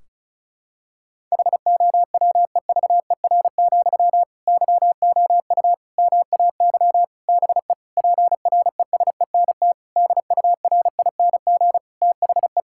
Morse code, heard as a keyed tone, about 34 wpm, 700 Hz. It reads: HOWEVER, YOU MAY BE PRESENT DURING THE
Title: Sentences from Top 1600 Words 35wpm